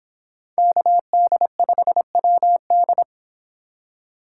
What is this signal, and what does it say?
Morse code, heard as a keyed tone, about 25 wpm, 700 Hz. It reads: KD5WD